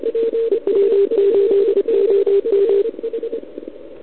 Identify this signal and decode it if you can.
Morse code, heard as a keyed tone, about 29 wpm, 400 Hz. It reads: EW8OM
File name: ARCHIV / OKDX2019 / OKOM40 KX3